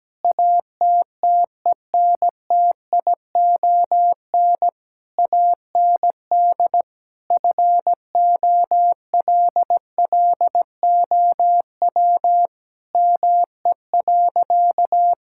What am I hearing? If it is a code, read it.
Morse code, heard as a keyed tone, about 17 wpm, 700 Hz. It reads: ATTENTION AND FOLLOW ME.
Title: Sentences from Top 1000 Words - Review Part 6 - 17wpm